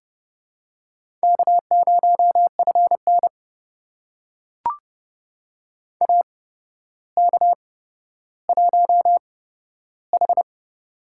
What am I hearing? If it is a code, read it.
Morse code, heard as a keyed tone, about 30 wpm, spaced as for 13 wpm, 700 Hz. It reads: K0FN AK1H
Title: USA Call Signs - 4 Characters - 8x Character Spacing - 30wpm